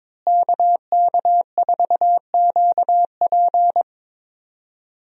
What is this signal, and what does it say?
Morse code, heard as a keyed tone, about 22 wpm, 700 Hz. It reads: KK4QP
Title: USA Call Signs - 5 Characters - Repeated 3x - 22wpm